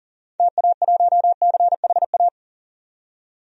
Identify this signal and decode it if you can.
Morse code, heard as a keyed tone, about 40 wpm, 700 Hz. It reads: TA1CHA
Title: DXCC-100 Call Signs - 6 Characters - 40wpm